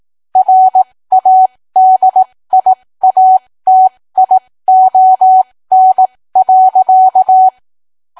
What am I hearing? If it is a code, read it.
Morse code, heard as a keyed tone, about 18 wpm, spaced as for 15 wpm, 750 Hz. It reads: RADIATION.